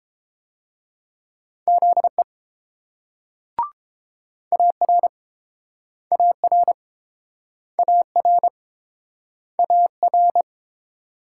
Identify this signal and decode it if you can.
Morse code, heard as a keyed tone, about 30 wpm, 700 Hz. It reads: ZE AR AR AR AR